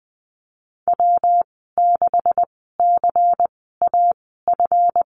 Morse code, heard as a keyed tone, about 20 wpm, spaced as for 14 wpm, 700 Hz. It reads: W6CAF
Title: 140 Lesson 23g - Letter f - Call Signs 20wpm